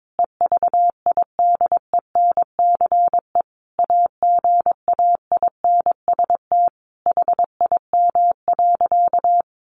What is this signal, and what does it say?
Morse code, heard as a keyed tone, about 22 wpm, 700 Hz. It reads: EVIDENCE AGAINST HIM.
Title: Sentences from Top 1300 Words 22wpm